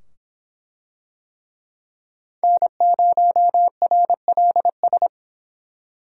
Morse code, heard as a keyed tone, about 25 wpm, 700 Hz. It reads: N0RLS